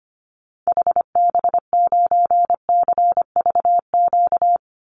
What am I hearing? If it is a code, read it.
Morse code, heard as a keyed tone, about 25 wpm, 700 Hz. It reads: HB9CVQ
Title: Call Signs Repeated 2x - 25wpm